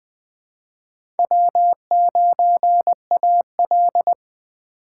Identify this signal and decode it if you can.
Morse code, heard as a keyed tone, about 20 wpm, 700 Hz. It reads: W9AL
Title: USA Call Signs - 4 Characters - 2x Character Spacing - 20wpm